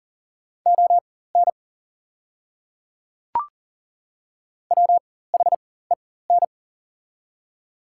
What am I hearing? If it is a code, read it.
Morse code, heard as a keyed tone, about 40 wpm, spaced as for 19 wpm, 690 Hz. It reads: ON WHEN
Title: 4x Character Spacing - Top 100 Words - 40wpm